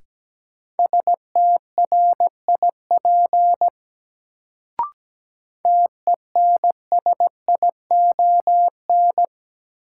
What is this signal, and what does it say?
Morse code, heard as a keyed tone, about 17 wpm, 700 Hz. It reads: STRIP TENSION
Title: Sentences from Top 1600 Words - New Word Drill Alt - 17wpm